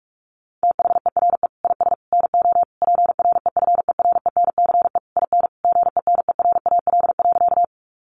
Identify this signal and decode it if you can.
Morse code, heard as a keyed tone, about 44 wpm, 700 Hz. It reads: THERE IS NO PREFERENCE IN GENERAL.
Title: Sentences from Top 1500 Words 45wpm